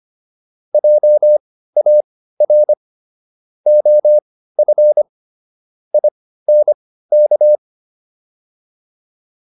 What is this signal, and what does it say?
Morse code, heard as a keyed tone, about 25 wpm, spaced as for 15 wpm, 600 Hz. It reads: JAR OF INK